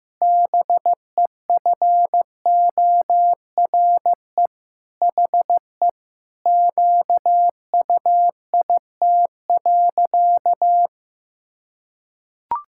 Morse code, heard as a keyed tone, about 15 wpm, 700 Hz. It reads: BEFORE HE QUIT.
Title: Pangram Challenge v2 - 15wpm